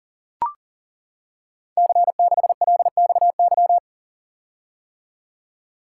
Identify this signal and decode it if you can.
Morse code, heard as a keyed tone, about 39 wpm, 700 Hz. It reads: C6LXY
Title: Code Groups - 5 Letter:Numbers - 40wpm